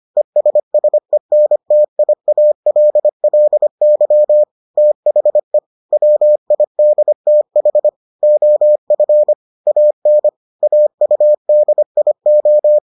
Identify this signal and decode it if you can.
Morse code, heard as a keyed tone, about 25 wpm, 600 Hz. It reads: ESSENTIALLY THE WIDTH OF AN AUDIO